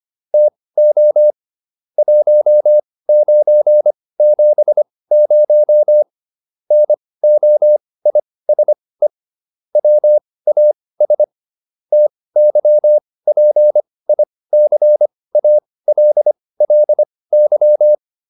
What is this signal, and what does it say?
Morse code, heard as a keyed tone, about 25 wpm, spaced as for 18 wpm, 600 Hz. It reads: TO 1970 NOISE WAS TYPICALLY